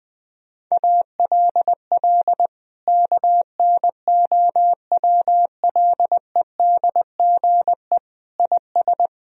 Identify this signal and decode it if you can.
Morse code, heard as a keyed tone, about 20 wpm, 700 Hz. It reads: ALL KNOWLEDGE IS
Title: Sentences from Top 1000 Words - Review Part 3 - 20wpm